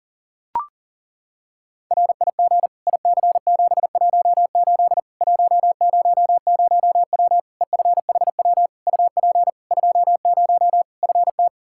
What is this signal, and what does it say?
Morse code, heard as a keyed tone, about 40 wpm, 700 Hz. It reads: RIG IC718 100W EFHW UP 20 FT
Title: Practice QSO - Exchange 2 - Info - 40wpm